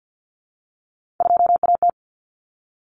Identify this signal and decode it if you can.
Morse code, heard as a keyed tone, about 50 wpm, 700 Hz. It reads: 2AT
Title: Common Cut Numbers - 3-Digit 50wpm